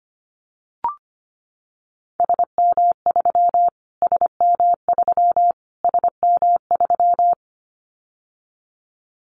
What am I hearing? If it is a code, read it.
Morse code, heard as a keyed tone, about 25 wpm, 700 Hz. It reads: SM3 SM3 SM3
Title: Callsign Prefixes for Top 25 DXCC - Repeated 3x - 25wpm